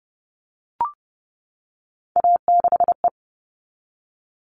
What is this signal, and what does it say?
Morse code, heard as a keyed tone, about 30 wpm, 700 Hz. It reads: A6E
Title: All Cut Numbers - 3-Digit 30wpm